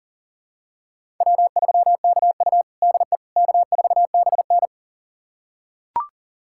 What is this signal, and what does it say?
Morse code, heard as a keyed tone, about 40 wpm, 700 Hz. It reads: W3KU DE K4BN
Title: USA Callsign DE Callsign - 4 Characters - 40wpm